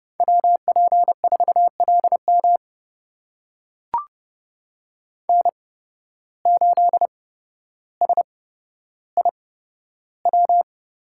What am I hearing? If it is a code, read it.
Morse code, heard as a keyed tone, about 30 wpm, spaced as for 13 wpm, 700 Hz. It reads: WP4LM N8SIW
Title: USA Call Signs - 5 Characters - 8x Character Spacing - 30wpm